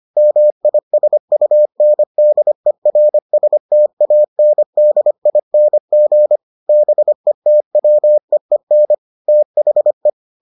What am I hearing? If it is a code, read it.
Morse code, heard as a keyed tone, about 25 wpm, 600 Hz. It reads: MISUNDERSTANDING BETWEEN THE